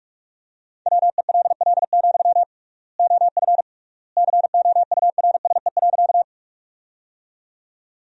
Morse code, heard as a keyed tone, about 46 wpm, 700 Hz. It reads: WELL, OF COURSE.
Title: 3x Word-Spacing - Sentences from Top 500 Words - 45wpm